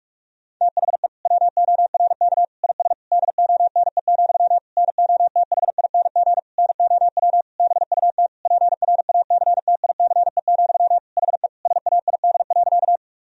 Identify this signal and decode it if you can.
Morse code, heard as a keyed tone, about 46 wpm, 700 Hz. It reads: THE WORK IS DONE, NOTHING NOW BUT PRACTICE, HE SAID.
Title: Sentences from Top 1000 Words - Review Part 4 - 45wpm